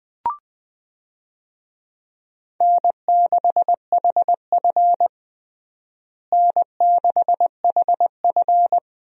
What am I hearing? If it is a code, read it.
Morse code, heard as a keyed tone, about 20 wpm, 700 Hz. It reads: N6HF N6HF
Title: USA Call Signs - 4 Characters - Repeated 2x - 20wpm